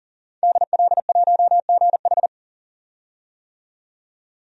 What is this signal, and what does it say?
Morse code, heard as a keyed tone, about 40 wpm, 700 Hz. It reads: DL1GH
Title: DXCC-100 Call Signs - 5 Characters - 40wpm